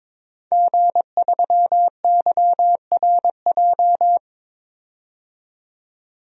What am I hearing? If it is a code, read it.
Morse code, heard as a keyed tone, about 21 wpm, 700 Hz. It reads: G3YRJ
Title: DXCC-100 Call Signs - 5 Characters - 22wpm